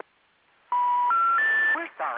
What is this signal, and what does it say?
Special information tone, with speech.